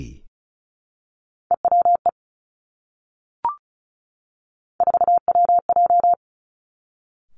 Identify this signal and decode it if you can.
Morse code, heard as a keyed tone, about 36 wpm, 700 Hz. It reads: EWE 4WJ
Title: Three Challenging Letter-Number Combinations 35wpm